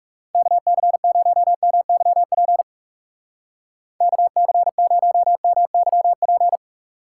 Morse code, heard as a keyed tone, about 37 wpm, 700 Hz. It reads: KC0MYP KC0MYP
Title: Speed Racing - USA 2x3 Call Signs 30wpm